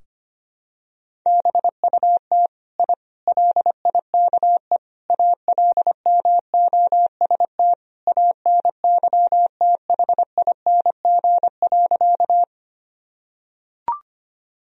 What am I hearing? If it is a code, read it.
Morse code, heard as a keyed tone, about 25 wpm, 700 Hz. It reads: BUT I LIKE ALMOST ANYTHING.
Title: Sentences from Top 1000 Words - Review Part 4 - 25wpm